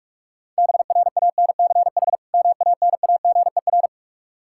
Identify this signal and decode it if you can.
Morse code, heard as a keyed tone, about 42 wpm, 700 Hz. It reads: BRANCH MANAGER
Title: Practice QSO - Exchange 3 - More Info - 45wpm